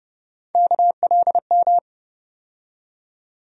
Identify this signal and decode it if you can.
Morse code, heard as a keyed tone, about 30 wpm, 700 Hz. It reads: KLM
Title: USA Call Sign Postfixes - 3 Letters 30wpm